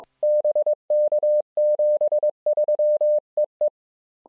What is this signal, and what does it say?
Morse code, heard as a keyed tone, about 22 wpm, 600 Hz. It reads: BK 73 EE
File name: je2pwn-g8bcg